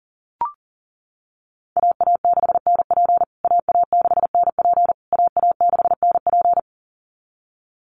Morse code, heard as a keyed tone, about 40 wpm, 700 Hz. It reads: AA6NP AA6NP AA6NP
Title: USA Call Signs Repeated 3x 40wpm